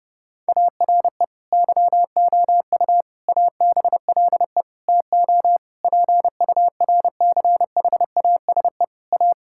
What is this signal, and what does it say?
Morse code, heard as a keyed tone, about 32 wpm, 700 Hz. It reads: ARE YOU ABLE TO PURCHASE A
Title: Sentences from Top 1000 Words - Review Part 5 - 30wpm